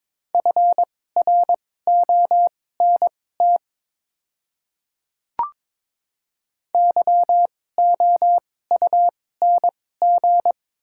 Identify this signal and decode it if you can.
Morse code, heard as a keyed tone, about 22 wpm, spaced as for 16 wpm, 700 Hz. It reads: FRONT YOUNG
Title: 2x Character Spacing - Top 500 Words - 22wpm